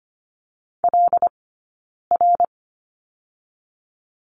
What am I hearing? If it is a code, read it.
Morse code, heard as a keyed tone, about 25 wpm, 700 Hz. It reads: L R
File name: course / 25:10wpm / 092 Lesson 16d - Letter c - Two Characters 25wpm